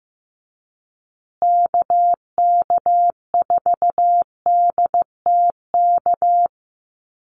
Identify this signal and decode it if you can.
Morse code, heard as a keyed tone, about 15 wpm, 700 Hz. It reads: KK4DTK
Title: USA Call Signs Repeated 2x 15wpm